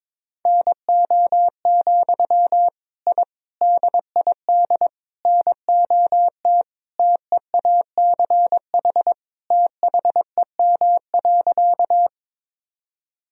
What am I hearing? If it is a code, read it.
Morse code, heard as a keyed tone, about 22 wpm, 700 Hz. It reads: NO, I DID NOT TEACH THEM.
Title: Sentences from Top 1000 Words - Review Part 2 - 22wpm